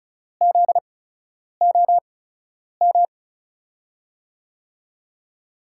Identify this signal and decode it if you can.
Morse code, heard as a keyed tone, about 36 wpm, spaced as for 10 wpm, 700 Hz. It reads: ZOM
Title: USA Call Signs - 8x Character Spacing - 35wpm